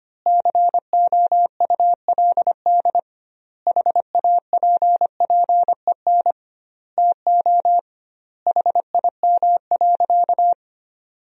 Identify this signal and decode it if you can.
Morse code, heard as a keyed tone, about 25 wpm, 700 Hz. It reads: COULD HAPPEN TO HIM.
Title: 2x Word-Spacing - Sentences from Top 400 Words - 25wpm